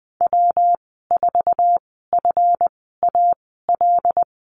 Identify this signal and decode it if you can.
Morse code, heard as a keyed tone, about 20 wpm, spaced as for 14 wpm, 700 Hz. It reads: W4FAL
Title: 140 Lesson 23g - Letter f - Call Signs 20wpm